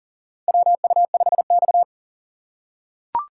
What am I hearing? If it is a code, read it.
Morse code, heard as a keyed tone, about 38 wpm, 700 Hz. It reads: WU5X
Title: USA Call Signs - 4 Characters - 2x Character Spacing - 40wpm